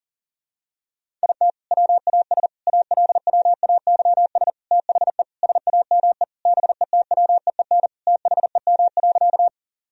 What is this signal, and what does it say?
Morse code, heard as a keyed tone, about 40 wpm, 700 Hz. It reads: IT WAS ALWAYS THE SAME BETWEEN THEM.
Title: Sentences from Top 1000 Words - Review Part 1 - 40wpm